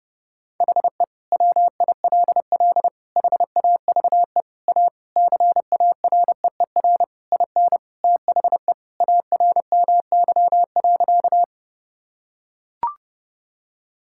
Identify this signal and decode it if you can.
Morse code, heard as a keyed tone, about 30 wpm, 700 Hz. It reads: HE WILL HAVE A CAREER IN THE ARMY.
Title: Sentences from Top 1000 Words - Review Part 6 - 30wpm